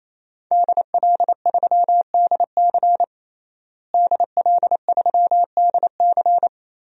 Morse code, heard as a keyed tone, about 28 wpm, 700 Hz. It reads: DL3DC DL3DC
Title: DXCC-100 Call Signs - 5 Characters - Repeated 3x - 28wpm